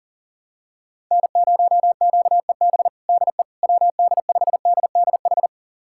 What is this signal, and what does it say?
Morse code, heard as a keyed tone, about 40 wpm, 700 Hz. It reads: N0QEB DE WD5DDH
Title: USA Callsign DE Callsign 40wpm